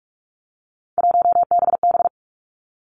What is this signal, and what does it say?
Morse code, heard as a keyed tone, about 46 wpm, 700 Hz. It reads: 1BB
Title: Three Challenging Letter-Number Combinations 45wpm